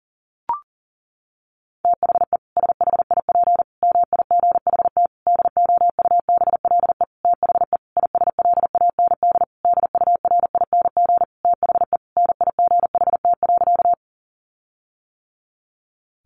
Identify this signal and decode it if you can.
Morse code, heard as a keyed tone, about 40 wpm, 700 Hz. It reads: THE SHIP MIGHT DOUBLE THE ISLAND DURING THE NIGHT.
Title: Sentences from Top 1400 Words 40wpm